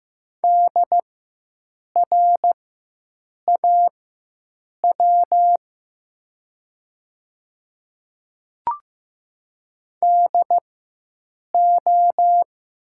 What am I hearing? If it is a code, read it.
Morse code, heard as a keyed tone, about 15 wpm, spaced as for 7 wpm, 700 Hz. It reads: DRAW DO
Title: 4x Character Spacing - Top 300 Words - 15wpm